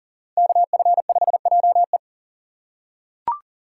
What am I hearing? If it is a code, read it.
Morse code, heard as a keyed tone, about 40 wpm, 700 Hz. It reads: KF5JE
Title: USA Call Signs - 5 Characters - 6x Character Spacing - 40wpm